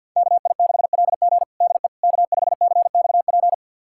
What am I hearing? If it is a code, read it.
Morse code, heard as a keyed tone, about 51 wpm, 700 Hz. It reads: KI6LG DE K5CXP
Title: USA Callsign DE Callsign - 5 Characters - 50wpm